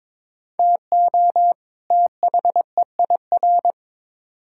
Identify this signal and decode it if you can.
Morse code, heard as a keyed tone, about 21 wpm, 700 Hz. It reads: TO THEIR
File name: Build A Phrase - 2-Word Phrases - 22wpm